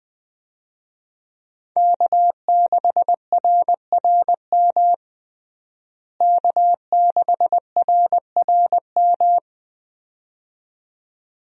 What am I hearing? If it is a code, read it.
Morse code, heard as a keyed tone, about 20 wpm, 700 Hz. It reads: K6RRM K6RRM
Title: USA Call Signs - 5 Characters - Repeated 2x - 20wpm